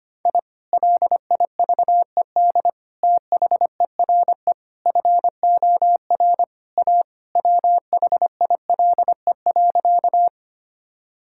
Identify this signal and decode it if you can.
Morse code, heard as a keyed tone, about 25 wpm, 700 Hz. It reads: I LIVED THERE FOR A WHILE.
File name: practice / 25wpm / Sentences with 3-Word Phrases - 25wpm